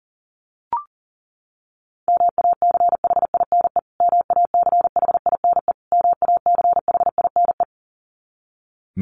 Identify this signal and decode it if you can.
Morse code, heard as a keyed tone, about 37 wpm, 700 Hz. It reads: MACHINE MACHINE MACHINE
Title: Top 500 Words Repeated 40wpm